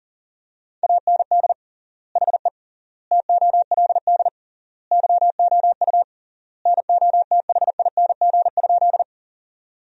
Morse code, heard as a keyed tone, about 37 wpm, 700 Hz. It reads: AND HE TOLD YOU NOTHING?
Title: 3x Word-Spacing - Sentences from Top 500 Words - 40wpm